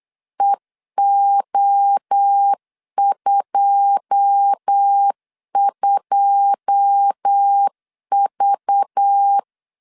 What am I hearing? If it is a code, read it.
Morse code, heard as a keyed tone, about 8 wpm, 800 Hz. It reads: EO22V
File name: LEZIONI / 9 2 8wpm 3437